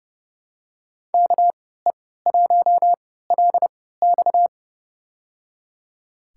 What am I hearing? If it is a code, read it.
Morse code, heard as a keyed tone, about 32 wpm, spaced as for 17 wpm, 700 Hz. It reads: KE1LX
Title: USA Call Signs - 3x Character Spacing - 30wpm